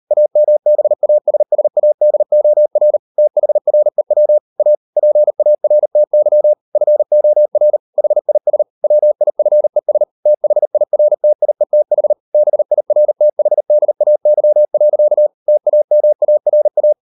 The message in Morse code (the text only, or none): AMBASSADOR THREW A PARTY FOR HIS WIFES THIRTIETH BIRTHDAY. TAMARA